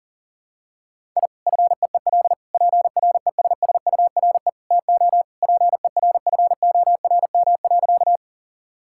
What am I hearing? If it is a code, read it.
Morse code, heard as a keyed tone, about 40 wpm, 700 Hz. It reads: I FEEL PRESSURE TO PERFORM.
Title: Sentences from Top 1000 Words - Review Part 6 - 40wpm